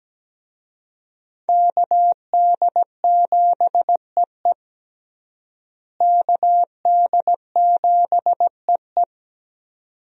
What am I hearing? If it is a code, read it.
Morse code, heard as a keyed tone, about 17 wpm, 700 Hz. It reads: KD7EE KD7EE
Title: USA Call Signs - 5 Characters - Repeated 2x - 17wpm